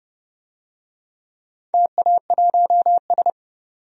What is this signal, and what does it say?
Morse code, heard as a keyed tone, about 30 wpm, 700 Hz. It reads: TA1S
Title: DXCC-100 Call Signs - 4 Characters - 30wpm